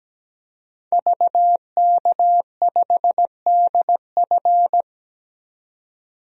Morse code, heard as a keyed tone, about 17 wpm, 700 Hz. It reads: VK5DF
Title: DXCC-100 Call Signs - 5 Characters - 17wpm